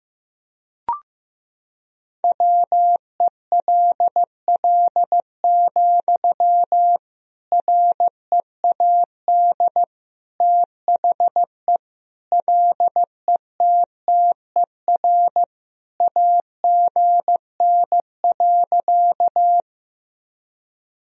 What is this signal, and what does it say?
Morse code, heard as a keyed tone, about 15 wpm, 700 Hz. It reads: WELL, READ THE LETTER AGN.
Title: Sentences from Top 300 Words and Abbreviations 15wpm